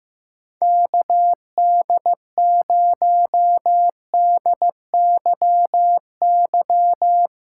Morse code, heard as a keyed tone, about 15 wpm, 700 Hz. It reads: KD0DYY
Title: USA Callsign DE Callsign - 6 Characters - 15wpm